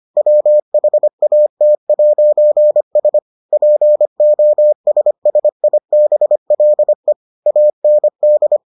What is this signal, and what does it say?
Morse code, heard as a keyed tone, about 25 wpm, 600 Hz. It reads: WHAT'S POSSIBLE AND